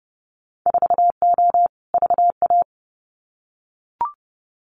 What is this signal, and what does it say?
Morse code, heard as a keyed tone, about 31 wpm, 700 Hz. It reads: 4O VA
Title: Winter Field Day Exchanges Encore 30wpm